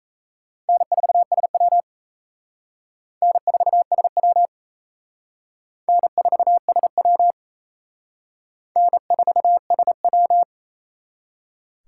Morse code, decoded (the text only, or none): N4SW N4SW N4SW N4SW